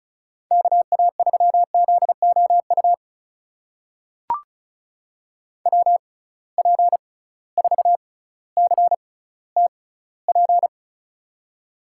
Morse code, heard as a keyed tone, about 36 wpm, spaced as for 18 wpm, 700 Hz. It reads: KA3ZOU WP4CTP
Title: USA Call Signs - 6 Characters - 6x Character Spacing - 35wpm